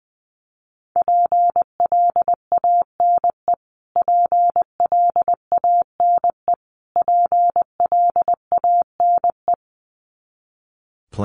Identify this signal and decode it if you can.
Morse code, heard as a keyed tone, about 20 wpm, 700 Hz. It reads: PLANE PLANE PLANE
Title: Top 500 Words Repeated 20wpm